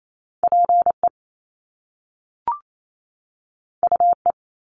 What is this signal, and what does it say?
Morse code, heard as a keyed tone, about 28 wpm, 700 Hz. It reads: PE UE
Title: Top Two Letter Combinations in Words 28wpm